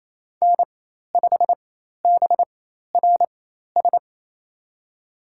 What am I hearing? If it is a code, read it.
Morse code, heard as a keyed tone, about 28 wpm, spaced as for 13 wpm, 700 Hz. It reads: N5BRS
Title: USA Call Signs - 5 Characters - 4x Character Spacing - 28wpm